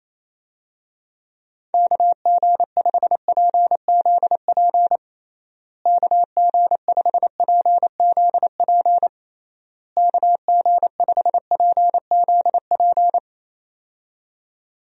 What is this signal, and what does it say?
Morse code, heard as a keyed tone, about 28 wpm, 700 Hz. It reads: KG5PZP KG5PZP KG5PZP
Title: USA Call Signs - 6 Characters - Repeated 3x - 28wpm